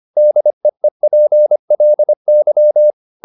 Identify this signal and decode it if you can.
Morse code, heard as a keyed tone, about 25 wpm, 600 Hz. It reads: DEEPLY